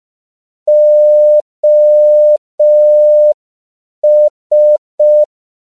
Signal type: Morse code